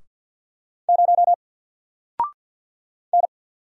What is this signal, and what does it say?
Morse code, heard as a keyed tone, about 48 wpm, 700 Hz. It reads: 0 N
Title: Single Letter-Number Alt 50wpm